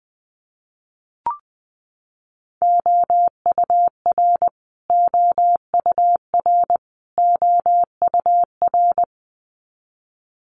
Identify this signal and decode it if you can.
Morse code, heard as a keyed tone, about 20 wpm, 700 Hz. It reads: OUR OUR OUR
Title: Top 500 Words Repeated 20wpm